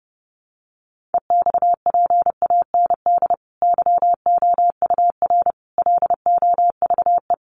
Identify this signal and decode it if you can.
Morse code, heard as a keyed tone, about 30 wpm, 700 Hz. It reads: EXPAND YOUR LOVE
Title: Sentences from Top 1400 Words 30wpm